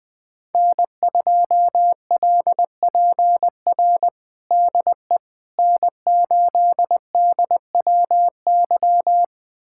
Morse code, heard as a keyed tone, about 21 wpm, 700 Hz. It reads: N2LPR DE N8DWY
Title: USA Callsign DE Callsign - 5 Characters - 20wpm